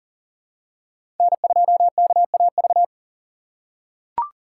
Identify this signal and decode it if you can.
Morse code, heard as a keyed tone, about 40 wpm, 700 Hz. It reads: N2KAV
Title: USA Call Signs - 3x Character Spacing - 40wpm